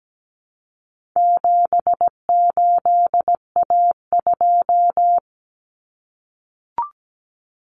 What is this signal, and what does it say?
Morse code, heard as a keyed tone, about 17 wpm, 700 Hz. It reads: 78A2
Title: Common Cut Numbers - 4-Digit 17wpm